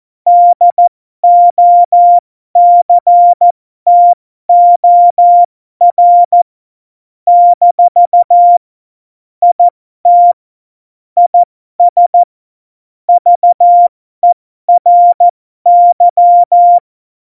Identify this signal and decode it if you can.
Morse code, heard as a keyed tone, about 14 wpm, spaced as for 12 wpm, 700 Hz. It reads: DOCTOR - IT IS VERY
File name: Morsekurs / Witze